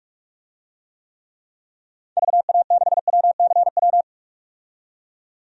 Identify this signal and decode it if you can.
Morse code, heard as a keyed tone, about 43 wpm, 700 Hz. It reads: VA6WCW
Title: DXCC-100 Call Signs - 6 Characters - 45wpm